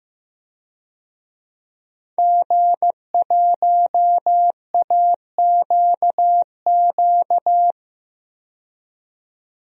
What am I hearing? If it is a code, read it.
Morse code, heard as a keyed tone, about 15 wpm, 700 Hz. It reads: G1AQQ
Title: DXCC-100 Call Signs - 5 Characters - 15wpm